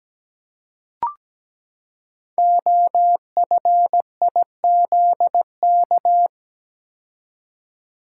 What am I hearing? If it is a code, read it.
Morse code, heard as a keyed tone, about 17 wpm, 700 Hz. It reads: OFIZK
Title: Code Groups - 5 Letters - 17wpm